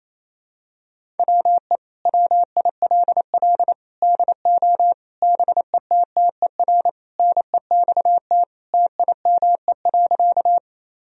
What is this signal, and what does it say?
Morse code, heard as a keyed tone, about 28 wpm, 700 Hz. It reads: WE WILL DO BETTER NEXT TIME.
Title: Sentences from Top 1000 Words - Review Part 1 - 28wpm